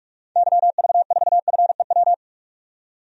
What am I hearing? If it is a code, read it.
Morse code, heard as a keyed tone, about 47 wpm, 700 Hz. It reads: YV4FEW
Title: DXCC-100 Call Signs - 6 Characters - 45wpm